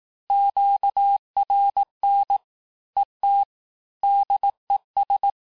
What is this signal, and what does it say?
Morse code, heard as a keyed tone, about 18 wpm, 800 Hz. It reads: QRN ET DES
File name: cours-du-lundi files / cours-1852-FIN-du-QSO-vitesse-17MPM